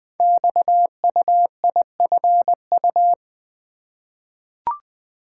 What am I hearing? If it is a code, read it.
Morse code, heard as a keyed tone, about 20 wpm, 700 Hz. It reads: XUIFU
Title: Code Groups - 5 Letters - 20wpm